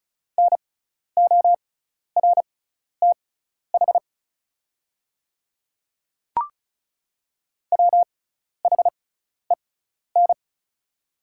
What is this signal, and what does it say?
Morse code, heard as a keyed tone, about 34 wpm, spaced as for 12 wpm, 700 Hz. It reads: NORTH WHEN
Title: 6x Character Spacing - Top 300 Words - 35wpm